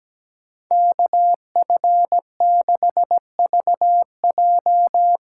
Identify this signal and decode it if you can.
Morse code, heard as a keyed tone, about 17 wpm, 700 Hz. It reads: KF6VJ
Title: USA Call Signs - 6 Characters - 8x Character Spacing - 17wpm